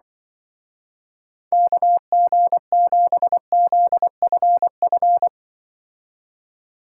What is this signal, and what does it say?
Morse code, heard as a keyed tone, about 24 wpm, 700 Hz. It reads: KG7ZFF
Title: Speed Racing - USA 2x3 Call Signs 20wpm